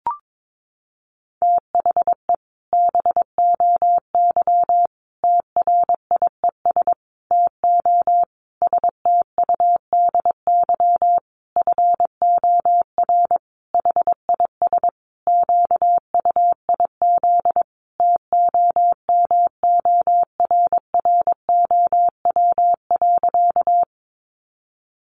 Morse code, heard as a keyed tone, about 22 wpm, 700 Hz. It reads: THE BOY TRIES TO STUDY FOR HIS QUIZ TOMORROW.
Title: Additional Easy Phrases 22wpm